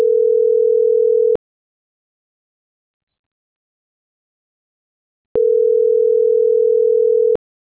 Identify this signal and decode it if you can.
Ringback tone (North America), 440 + 480 Hz.